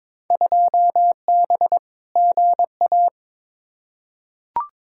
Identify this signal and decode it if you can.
Morse code, heard as a keyed tone, about 22 wpm, 700 Hz. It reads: 2B GA
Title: Field Day Exchanges Encore - 22wpm